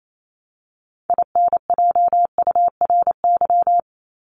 Morse code, heard as a keyed tone, about 28 wpm, 700 Hz. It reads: INJURY
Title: Top 1751-2000 Words 28wpm